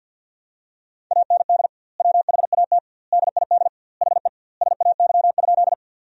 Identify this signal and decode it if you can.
Morse code, heard as a keyed tone, about 53 wpm, 700 Hz. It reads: AND WHAT DID HE SAY?
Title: 2x Word-Spacing - Sentences from Top 200 Words - 50wpm